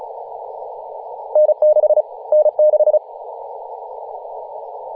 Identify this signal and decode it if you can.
Morse code, heard as a keyed tone, about 34 wpm, 610 Hz, with radio noise. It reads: N6 N6
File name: posts / 2017-nov-04-2233z-vk9ma-21-025mhz-cw